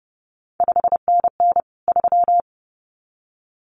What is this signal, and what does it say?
Morse code, heard as a keyed tone, about 31 wpm, 700 Hz. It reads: 5NN 3
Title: CQ WW DX Contest Exchange - 30wpm